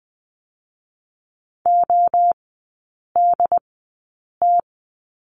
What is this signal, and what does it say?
Morse code, heard as a keyed tone, about 20 wpm, spaced as for 8 wpm, 700 Hz. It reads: ODT
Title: 067 Lesson 12e - Letter d - Three Characters 20wpm